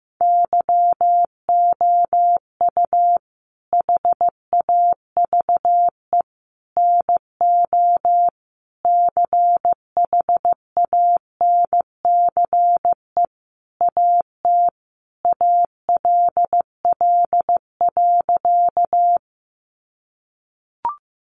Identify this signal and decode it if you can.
Morse code, heard as a keyed tone, about 15 wpm, 700 Hz. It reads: YOU HAVE NO CHANCE AT ALL.
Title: Sentences from Top 800 Words 15wpm